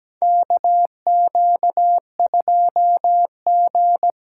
Morse code, heard as a keyed tone, about 17 wpm, 700 Hz. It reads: KQ2G
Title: USA Call Signs - 4 Characters - 3x Character Spacing - 17wpm